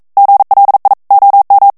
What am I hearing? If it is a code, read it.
Morse code, heard as a keyed tone, about 42 wpm, 800 Hz. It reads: GLI OM
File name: morse mp3 / ITALIEN texte clair 1 vitesse 40 mots minute